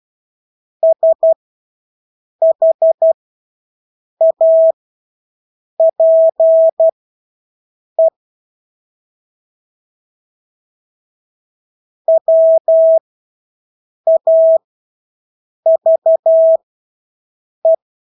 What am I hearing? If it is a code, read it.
Morse code, heard as a keyed tone, about 12 wpm, spaced as for 6 wpm, 650 Hz. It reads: SHAPE WAVE
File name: Engelske ord 12-6 wpm